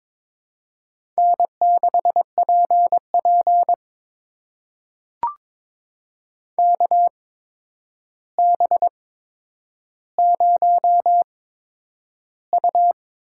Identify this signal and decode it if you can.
Morse code, heard as a keyed tone, about 22 wpm, spaced as for 9 wpm, 700 Hz. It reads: N6PP KB0U
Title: USA Call Signs - 8x Character Spacing - 22wpm